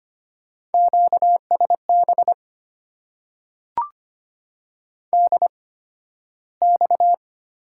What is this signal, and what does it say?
Morse code, heard as a keyed tone, about 25 wpm, spaced as for 13 wpm, 700 Hz. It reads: QSB DX
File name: Top 90 QSO Elements - 8x Character Spacing - 25wpm